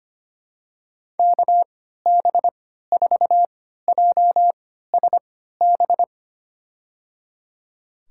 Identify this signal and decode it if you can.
Morse code, heard as a keyed tone, about 25 wpm, spaced as for 14 wpm, 700 Hz. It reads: KB4JSB
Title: USA Call Signs - 6 Characters - 3x Character Spacing - 25wpm